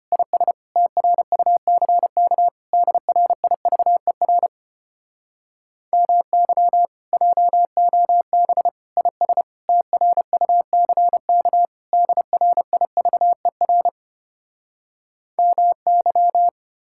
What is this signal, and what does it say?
Morse code, heard as a keyed tone, about 33 wpm, 700 Hz. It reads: IS TRUCK DRIVER MY JOB IS TRUCK DRIVER MY